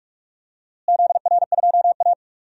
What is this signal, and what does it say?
Morse code, heard as a keyed tone, about 41 wpm, 700 Hz. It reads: ZR2A
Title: DXCC-100 Call Signs - 4 Characters - Repeated 3x - 45wpm